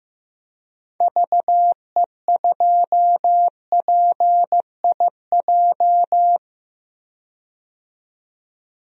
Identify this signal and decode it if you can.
Morse code, heard as a keyed tone, about 15 wpm, 700 Hz. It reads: VE2PIJ
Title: DXCC-100 Call Signs - 6 Characters - 15wpm